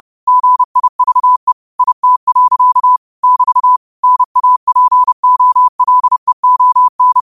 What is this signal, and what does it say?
Morse code, heard as a keyed tone, about 30 wpm, 1000 Hz. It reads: GIVE IT. <BT> NAPOLEON